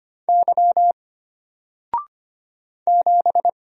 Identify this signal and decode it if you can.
Morse code, heard as a keyed tone, about 25 wpm, 700 Hz. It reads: Y 7
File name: Single-Letter-Number-25wpm